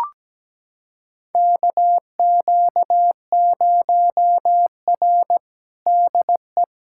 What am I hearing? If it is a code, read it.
Morse code, heard as a keyed tone, about 17 wpm, 700 Hz. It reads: KQ0R DE
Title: USA Callsign DE Callsign 17wpm